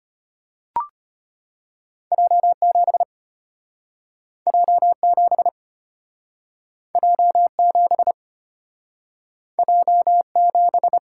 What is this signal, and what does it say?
Morse code, heard as a keyed tone, about 34 wpm, 690 Hz. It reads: J7 J7 J7 J7